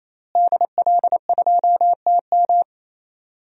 Morse code, heard as a keyed tone, about 28 wpm, 700 Hz. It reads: DL2TM
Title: DXCC-100 Call Signs - 5 Characters - Repeated 3x - 28wpm